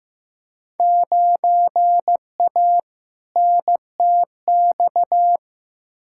Morse code, heard as a keyed tone, about 15 wpm, 700 Hz. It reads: 9A NTX
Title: Field Day Exchanges Encore - 15wpm